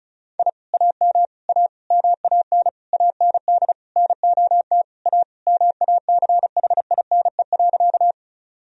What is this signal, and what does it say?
Morse code, heard as a keyed tone, about 34 wpm, 700 Hz. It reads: I AM A MAN AND NOT A MACHINE.
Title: Sentences from Top 1000 Words - Review Part 2 - 35wpm